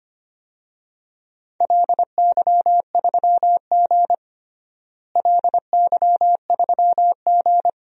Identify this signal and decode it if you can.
Morse code, heard as a keyed tone, about 25 wpm, 700 Hz. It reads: LY3G LY3G
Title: DXCC-100 Call Signs - 4 Characters - Repeated 2x - 25wpm